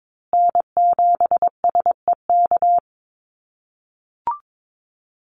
Morse code, heard as a keyed tone, about 22 wpm, 700 Hz. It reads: N7SEK
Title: USA Call Signs Repeated 2x 22wpm